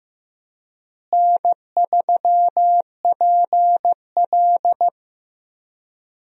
Morse code, heard as a keyed tone, about 15 wpm, 700 Hz. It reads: N3PL
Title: USA Call Signs - 4 Characters - Repeated 3x - 15wpm